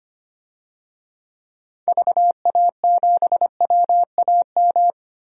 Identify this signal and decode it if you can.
Morse code, heard as a keyed tone, about 25 wpm, 700 Hz. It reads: VA7WAM
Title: DXCC-100 Call Signs - 6 Characters - 25wpm